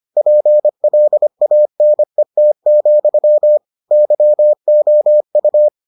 Morse code, heard as a keyed tone, about 25 wpm, 600 Hz. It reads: PLANET, YOU